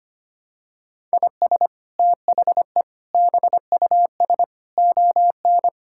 Morse code, heard as a keyed tone, about 25 wpm, 700 Hz. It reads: IS THE BUS ON
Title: Sentences from Top 1000 Words - Review Part 6 - 25wpm